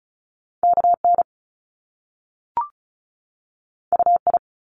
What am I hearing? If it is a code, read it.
Morse code, heard as a keyed tone, about 36 wpm, 700 Hz. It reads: KN UI
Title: Top Two Letter Combinations in Words 35wpm